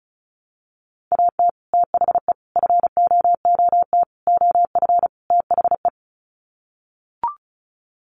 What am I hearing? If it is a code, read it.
Morse code, heard as a keyed tone, about 34 wpm, 700 Hz. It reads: AT THE FOOT OF THE
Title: Sets of 5 Words 35wpm